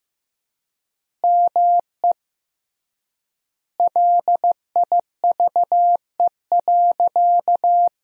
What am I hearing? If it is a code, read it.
Morse code, heard as a keyed tone, about 15 wpm, 700 Hz. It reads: ME LIVE.
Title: 3x Word-Spacing - Sentences from Top 300 Words - 15wpm